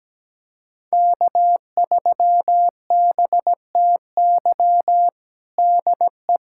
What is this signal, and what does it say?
Morse code, heard as a keyed tone, about 17 wpm, 700 Hz. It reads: K3BTY DE
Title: USA Callsign DE Callsign - 5 Characters - 17wpm